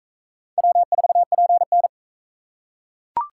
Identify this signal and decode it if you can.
Morse code, heard as a keyed tone, about 41 wpm, 700 Hz. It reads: W4PN